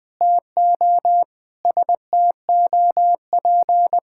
Morse code, heard as a keyed tone, about 20 wpm, 700 Hz. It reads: TO STOP
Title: Sets of 8 Words - 20wpm